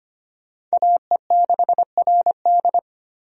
Morse code, heard as a keyed tone, about 25 wpm, 700 Hz. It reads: AE6RD
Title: USA Call Signs - 5 Characters - 25wpm